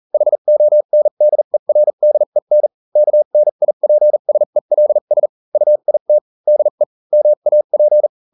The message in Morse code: HONDERDEN KNIPSELS UIT DE MAP